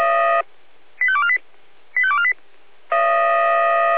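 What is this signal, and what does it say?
Five-tone selective call (ZVEI), heard as quick runs of five tones, tones of about 70 ms. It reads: 85248 85248
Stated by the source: ZVEI five-tone selective call received on SDR.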